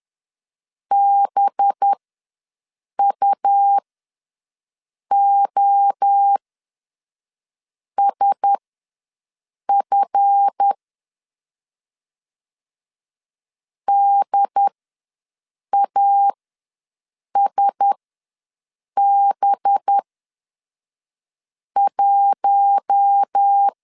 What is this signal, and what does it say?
Morse code, heard as a keyed tone, about 10 wpm, spaced as for 5 wpm, 800 Hz. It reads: BUOSF DASB1
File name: LEZIONI / 7 GF9Z / 7 GF9Z ws 10wpm 3919